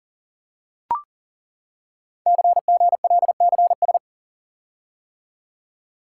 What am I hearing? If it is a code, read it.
Morse code, heard as a keyed tone, about 37 wpm, 700 Hz. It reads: CGLCS
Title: Code Groups - 5 Letters - 40wpm